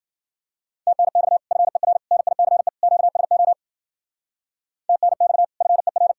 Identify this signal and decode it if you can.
Morse code, heard as a keyed tone, about 52 wpm, 700 Hz. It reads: TNX FER NICE QSO TNX FER